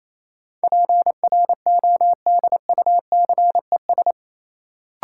Morse code, heard as a keyed tone, about 28 wpm, 700 Hz. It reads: PRODUCES
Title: Sentences from Top 1000 Words - Review Part 6 - New Word Drill - 28wpm